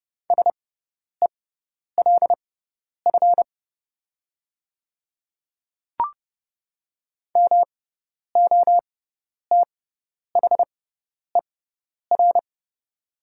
Morse code, heard as a keyed tone, about 30 wpm, spaced as for 10 wpm, 700 Hz. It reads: SELF MOTHER
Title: 6x Character Spacing - Top 300 Words - 30wpm